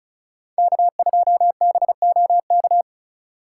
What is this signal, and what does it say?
Morse code, heard as a keyed tone, about 43 wpm, 700 Hz. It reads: K2BOK
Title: Speed Racing - USA Call Signs 35wpm